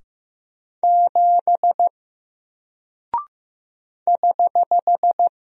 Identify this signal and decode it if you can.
Morse code, heard as a keyed tone, about 15 wpm, 700 Hz. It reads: 7 <HH>
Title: Single Character (Letter:Number:Punctuation:Prosign) Alt 15wpm